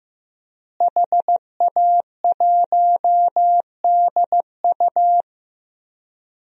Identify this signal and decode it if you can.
Morse code, heard as a keyed tone, about 15 wpm, 700 Hz. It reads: HA1DU
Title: DXCC-100 Call Signs - 5 Characters - Repeated 2x - 15wpm